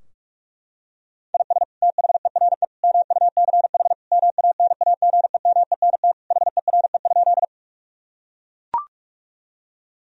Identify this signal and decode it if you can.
Morse code, heard as a keyed tone, about 41 wpm, 700 Hz. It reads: IS THERE MUCH MANAGEMENT HERE?
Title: Sentences from Top 1000 Words - Review Part 4 - 45wpm